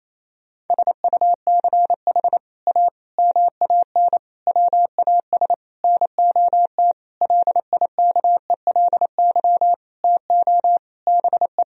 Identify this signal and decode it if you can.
Morse code, heard as a keyed tone, about 28 wpm, 700 Hz. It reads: SUCH A MAN WAS NOT LIKELY TO BE
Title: Sentences from Top 1000 Words - Review Part 4 - 28wpm